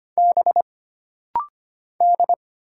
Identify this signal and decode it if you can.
Morse code, heard as a keyed tone, about 25 wpm, 700 Hz. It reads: B D
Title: Single Letter-Number Alt - Rapid-Fire 25wpm